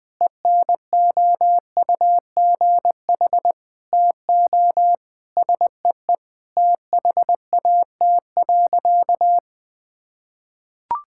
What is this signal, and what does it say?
Morse code, heard as a keyed tone, about 20 wpm, 700 Hz. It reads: ENOUGH TO SEE THAT.
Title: Sentences from Top 1600 Words 20wpm